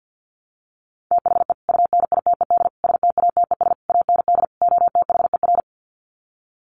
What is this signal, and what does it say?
Morse code, heard as a keyed tone, about 49 wpm, 700 Hz. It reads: THE UNITED STATES AND OTHER
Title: Sets of 5 Words Encore 50wpm